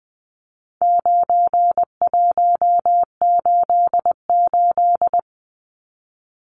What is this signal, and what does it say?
Morse code, heard as a keyed tone, about 20 wpm, 700 Hz. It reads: 9188
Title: Numbers - 4-Digit 20wpm